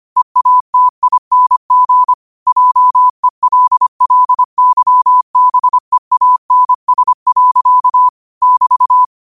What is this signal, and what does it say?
Morse code, heard as a keyed tone, about 25 wpm, 1000 Hz. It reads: EATING JELLYBEANS. <BT>
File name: qotdfiles / 25wpm